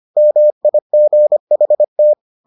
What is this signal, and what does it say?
Morse code, heard as a keyed tone, about 25 wpm, 600 Hz. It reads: MIGHT